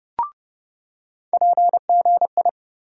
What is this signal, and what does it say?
Morse code, heard as a keyed tone, about 30 wpm, 700 Hz. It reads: PGI
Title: USA Call Sign Postfixes - 3 Letters 30wpm